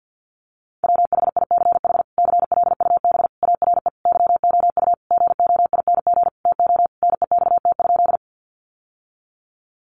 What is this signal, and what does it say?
Morse code, heard as a keyed tone, about 50 wpm, 700 Hz. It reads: WHICH CLUB ARE YOU GOING TO NEXT?
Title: Sentences from Top 1200 Words 50wpm